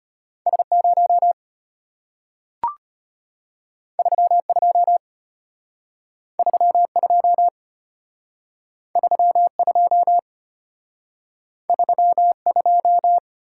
Morse code, heard as a keyed tone, about 34 wpm, 700 Hz. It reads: S0 32 32 32 32